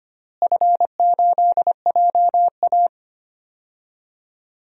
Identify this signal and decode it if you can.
Morse code, heard as a keyed tone, about 25 wpm, 700 Hz. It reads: F8JA
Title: DXCC-100 Call Signs - 4 Characters - Repeated 3x - 25wpm